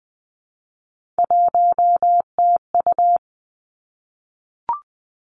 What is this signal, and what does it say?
Morse code, heard as a keyed tone, about 19 wpm, 700 Hz. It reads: 1TU
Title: Three Challenging Letter-Number Combinations 20wpm